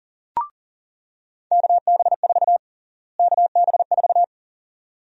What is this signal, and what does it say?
Morse code, heard as a keyed tone, about 40 wpm, 700 Hz. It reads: KB4 KB4
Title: USA Call Sign Prefixes - Repeated 2x - 40wpm